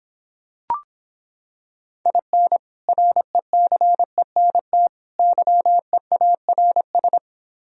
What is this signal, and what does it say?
Morse code, heard as a keyed tone, about 26 wpm, 700 Hz. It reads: IN RECENT YEARS